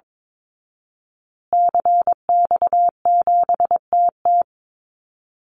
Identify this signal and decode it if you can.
Morse code, heard as a keyed tone, about 22 wpm, 700 Hz. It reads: CX7TT
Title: Call Signs Repeated 3x - 22wpm